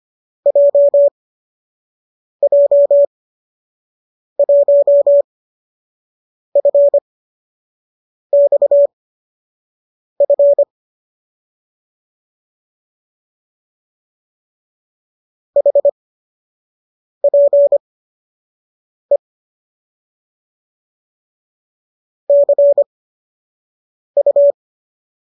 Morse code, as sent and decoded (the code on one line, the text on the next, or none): .--- .--- .---- ..-. -..- ..-. / .... .--. / -.-. ..-
JJ1FXF HPE CU